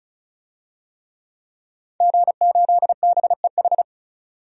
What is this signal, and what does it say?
Morse code, heard as a keyed tone, about 36 wpm, 700 Hz. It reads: G8BEH
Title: DXCC-100 Call Signs - 5 Characters - 35wpm